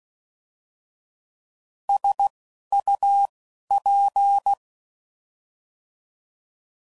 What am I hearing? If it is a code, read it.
Morse code, heard as a keyed tone, about 16 wpm, 800 Hz. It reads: S U P
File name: Prefixes / By First Letter / Ss